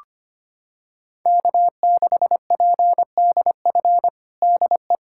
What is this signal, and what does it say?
Morse code, heard as a keyed tone, about 25 wpm, 700 Hz. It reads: K6PDF DE
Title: USA Callsign DE Callsign - 5 Characters - 25wpm